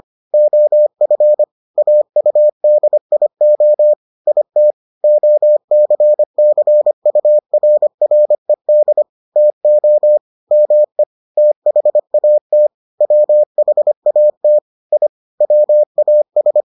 Morse code, heard as a keyed tone, about 25 wpm, 600 Hz. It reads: OF AUDIO IT OCCURRED TO ME THAT WHAT I WAS